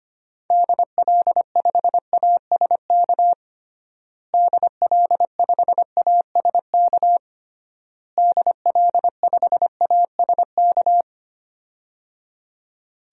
Morse code, heard as a keyed tone, about 25 wpm, 700 Hz. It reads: DL5ASK DL5ASK DL5ASK
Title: DXCC-100 Call Signs - 6 Characters - Repeated 3x - 25wpm